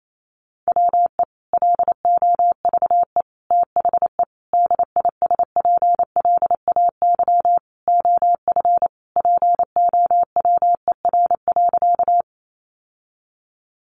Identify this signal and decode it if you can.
Morse code, heard as a keyed tone, about 28 wpm, 700 Hz. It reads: WE LOVE THE DISPLAY OF POWER.
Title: Sentences from Top 1200 Words 28wpm